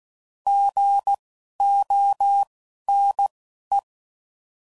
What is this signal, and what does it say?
Morse code, heard as a keyed tone, about 16 wpm, spaced as for 11 wpm, 800 Hz. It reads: GONE
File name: Prefixes / By First Letter